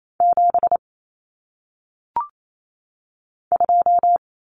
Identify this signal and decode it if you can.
Morse code, heard as a keyed tone, about 28 wpm, 700 Hz. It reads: 7 2
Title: Common Cut Numbers - 1-Digit 28wpm